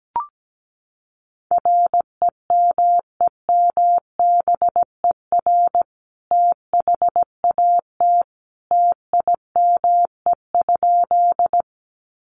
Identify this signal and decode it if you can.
Morse code, heard as a keyed tone, about 17 wpm, 700 Hz. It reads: REMEMBER THAT TIME?
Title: Sentences from Top 400 Words 17wpm